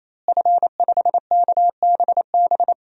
Morse code, heard as a keyed tone, about 28 wpm, 700 Hz. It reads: F5KBB
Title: DXCC-100 Call Signs - 5 Characters - Repeated 3x - 28wpm